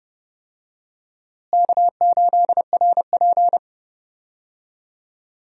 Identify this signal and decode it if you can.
Morse code, heard as a keyed tone, about 30 wpm, 700 Hz. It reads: K8RP
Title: USA Call Signs - 4 Characters - Repeated 3x - 30wpm